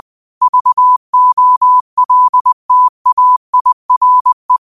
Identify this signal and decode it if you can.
Morse code, heard as a keyed tone, about 20 wpm, 1000 Hz. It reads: VOLTAIRE
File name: qotdfiles / 20wpm